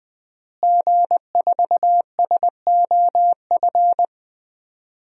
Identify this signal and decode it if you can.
Morse code, heard as a keyed tone, about 21 wpm, 700 Hz. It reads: G4SOF
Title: DXCC-100 Call Signs - 5 Characters - Repeated 2x - 20wpm